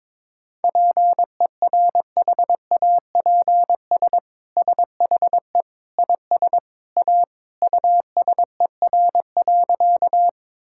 Morse code, heard as a keyed tone, about 22 wpm, 700 Hz. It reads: PERHAPS SHE IS A USER.
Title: Sentences from Top 1000 Words - Review Part 5 - 22wpm